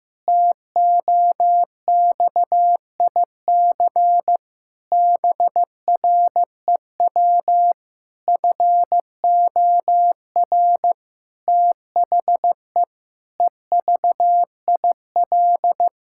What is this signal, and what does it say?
Morse code, heard as a keyed tone, about 16 wpm, 700 Hz. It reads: TOXIC BREW FOR THE EVIL
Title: Pangram Challenge v2 - 15wpm